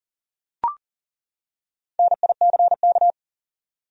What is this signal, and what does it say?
Morse code, heard as a keyed tone, about 40 wpm, 700 Hz. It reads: NICK
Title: USA Common Names - 40wpm